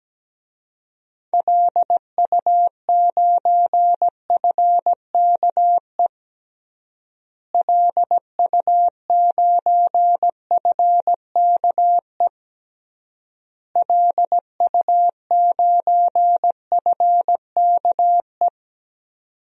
Morse code, heard as a keyed tone, about 17 wpm, 700 Hz. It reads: LU9FKE LU9FKE LU9FKE
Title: DXCC-100 Call Signs - 6 Characters - Repeated 3x - 17wpm